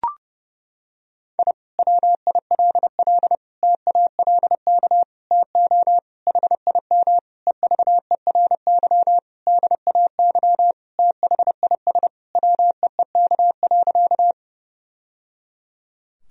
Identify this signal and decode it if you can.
Morse code, heard as a keyed tone, about 30 wpm, 700 Hz. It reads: I WILL TALK TO HIM EVERY DAY THIS WEEK.
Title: Sentences from Top 1000 Words - Review Part 2 - 30wpm